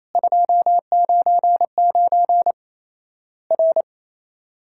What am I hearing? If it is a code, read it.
Morse code, heard as a keyed tone, about 27 wpm, 700 Hz. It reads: 299 R
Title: Worked All Europe DX Contest - QTC Exchange - 28wpm